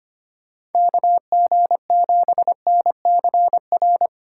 Morse code, heard as a keyed tone, about 30 wpm, 700 Hz. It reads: KG7NCR
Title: Speed Racing - USA 2x3 Call Signs 25wpm